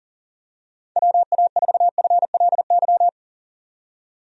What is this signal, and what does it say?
Morse code, heard as a keyed tone, about 37 wpm, 700 Hz. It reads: WA4FLY
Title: USA Call Signs - 6 Characters - 40wpm